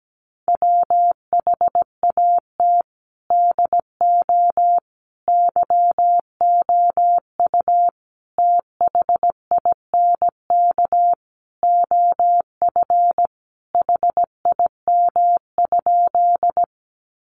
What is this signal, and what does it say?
Morse code, heard as a keyed tone, about 17 wpm, 700 Hz. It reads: WHAT DO YOU THINK OF HIM?
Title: Sentences from Top 200 Words 17wpm